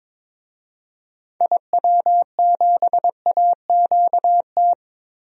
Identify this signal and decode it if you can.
Morse code, heard as a keyed tone, about 22 wpm, 700 Hz. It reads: IW7AQT
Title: DXCC-100 Call Signs - 6 Characters - Repeated 2x - 22wpm